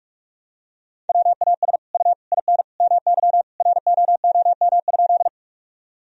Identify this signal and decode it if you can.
Morse code, heard as a keyed tone, about 43 wpm, 700 Hz. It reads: WAS U IN MY ROOM?
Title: Sentences from Top 300 Words and Abbreviations 45wpm